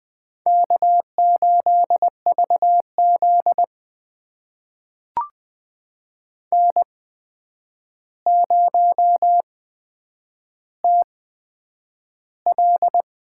Morse code, heard as a keyed tone, about 20 wpm, spaced as for 9 wpm, 700 Hz. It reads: K8VZ N0TL
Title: USA Call Signs - 4 Characters - 8x Character Spacing - 20wpm